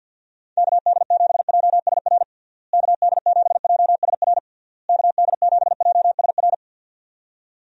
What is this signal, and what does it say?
Morse code, heard as a keyed tone, about 51 wpm, 700 Hz. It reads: KD7JSR KD7JSR KD7JSR
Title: USA Call Signs - 6 Characters - Repeated 3x - 50wpm